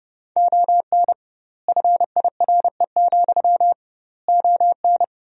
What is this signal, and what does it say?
Morse code, heard as a keyed tone, about 32 wpm, 700 Hz. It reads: ON FIRE, ON
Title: 2x Word-Spacing - Sentences from Top 400 Words - 30wpm